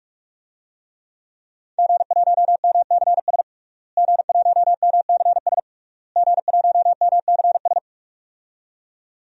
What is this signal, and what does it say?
Morse code, heard as a keyed tone, about 45 wpm, 700 Hz. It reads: G1MCS G1MCS G1MCS
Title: DXCC-100 Call Signs - 5 Characters - Repeated 3x - 45wpm